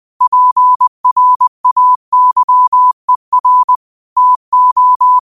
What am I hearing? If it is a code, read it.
Morse code, heard as a keyed tone, about 20 wpm, 1000 Hz. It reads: PRAYER TO